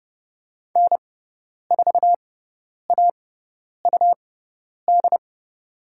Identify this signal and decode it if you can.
Morse code, heard as a keyed tone, about 30 wpm, spaced as for 10 wpm, 700 Hz. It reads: N4AUD
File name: Lesson 18g - Letter m - Call Signs 30wpm